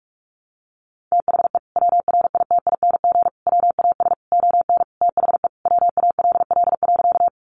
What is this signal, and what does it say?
Morse code, heard as a keyed tone, about 46 wpm, 700 Hz. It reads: THE WRITING WAS ON THE WALL.
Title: Sentences from Top 1100 Words 45wpm